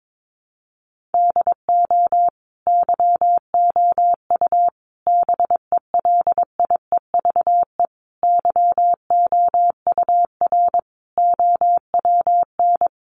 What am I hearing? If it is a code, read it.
Morse code, heard as a keyed tone, about 22 wpm, 700 Hz. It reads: DO YOU BELIEVE YOUR OWN
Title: Sentences from Top 1000 Words 22wpm